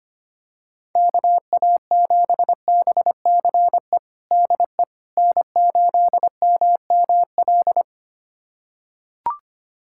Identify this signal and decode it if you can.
Morse code, heard as a keyed tone, about 25 wpm, 700 Hz. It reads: KA7BCE DE N8MML
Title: USA Callsign DE Callsign 25wpm